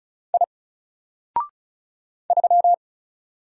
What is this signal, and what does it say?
Morse code, heard as a keyed tone, about 34 wpm, 700 Hz. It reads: I 3
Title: Single Letter-Number Alt 35wpm